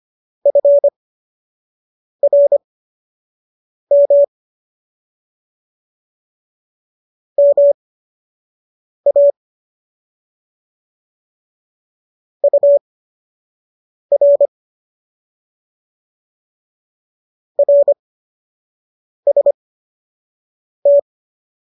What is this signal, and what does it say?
Morse code, heard as a keyed tone, about 25 wpm, spaced as for 6 wpm, 570 Hz. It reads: FRM MA UR RST